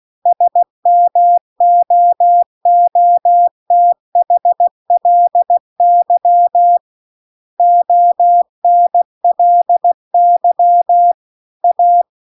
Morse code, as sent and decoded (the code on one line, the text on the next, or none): ... -- --- --- - .... .-.. -.-- / --- -. .-.. -.-- / .-
SMOOTHLY ONLY A